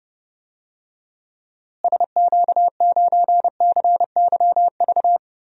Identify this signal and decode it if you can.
Morse code, heard as a keyed tone, about 30 wpm, 700 Hz. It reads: SQ9CYV
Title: DXCC-100 Call Signs - 6 Characters - Repeated 3x - 30wpm